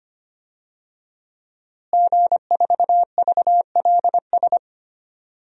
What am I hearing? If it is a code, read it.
Morse code, heard as a keyed tone, about 25 wpm, 700 Hz. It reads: G4VLS
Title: DXCC-100 Call Signs - 5 Characters - Repeated 2x - 25wpm